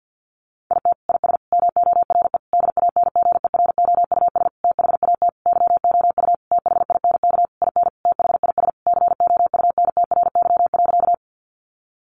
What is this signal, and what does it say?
Morse code, heard as a keyed tone, about 50 wpm, 700 Hz. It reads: IT IS MORE DANGEROUS THAT YOU THINK IN THIS COUNTRY.
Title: Sentences from Top 1200 Words 50wpm